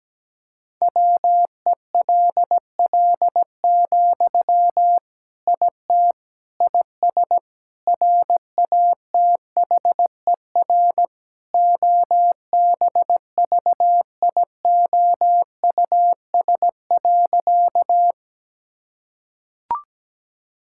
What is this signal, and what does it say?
Morse code, heard as a keyed tone, about 17 wpm, 700 Hz. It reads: WELL, IT IS RATHER OBVIOUS.
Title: Sentences from Top 1600 Words 17wpm